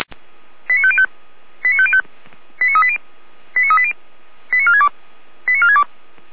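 Five-tone selective call (ZVEI), heard as quick runs of five tones, tones of about 70 ms. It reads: 89505 89505 89380 89380 89462 89462
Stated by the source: ZVEI five-tone selective call 89173, received on SDR by Feuerwehr Orlamünde.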